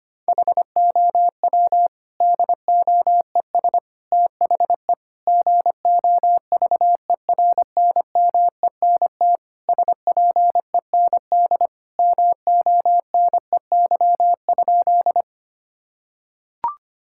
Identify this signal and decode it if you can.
Morse code, heard as a keyed tone, about 25 wpm, 700 Hz. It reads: HOW DOES THE GOVERNMENT SPEND MONEY?
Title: Sentences from Top 1000 Words - Review Part 5 - 25wpm